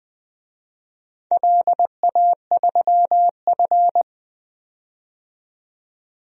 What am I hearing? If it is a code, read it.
Morse code, heard as a keyed tone, about 20 wpm, 700 Hz. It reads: LA3F
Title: DXCC-100 Call Signs - 4 Characters - 20wpm